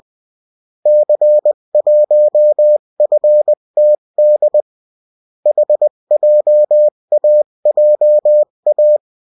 Morse code, heard as a keyed tone, about 20 wpm, 600 Hz. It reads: C1FTD HJAJA
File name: morse mp3 / revision9-vitesse 18 mots